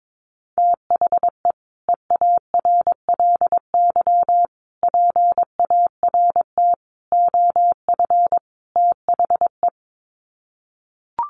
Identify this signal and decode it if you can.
Morse code, heard as a keyed tone, about 22 wpm, 700 Hz. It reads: THE EARLY PART OF THE
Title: Sets of 5 Words Encore2 22wpm